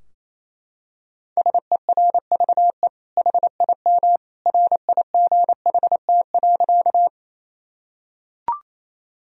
Morse code, heard as a keyed tone, about 28 wpm, 690 Hz. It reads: SERVE HIM RIGHT.
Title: Sentences from Top 1000 Words - Review Part 2 - 28wpm